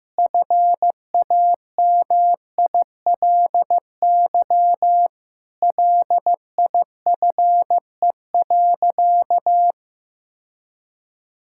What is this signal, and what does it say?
Morse code, heard as a keyed tone, about 15 wpm, 700 Hz. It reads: FAMILY LIFE.
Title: Sentences from Top 1000 Words - Review Part 5 - 15wpm